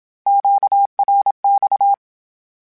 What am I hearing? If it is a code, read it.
Morse code, heard as a keyed tone, about 27 wpm, 800 Hz. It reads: QRX
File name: Q Calls / Q Calls-mp3s / Q Calls Twice @25